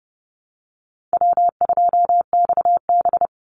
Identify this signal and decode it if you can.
Morse code, heard as a keyed tone, about 30 wpm, 700 Hz. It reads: W2XB
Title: Easy Call Signs - 30wpm